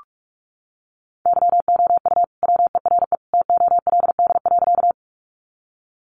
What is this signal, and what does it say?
Morse code, heard as a keyed tone, about 44 wpm, 700 Hz. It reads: YOU WERE TOLD.
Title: Sentences from Top 400 Words 45wpm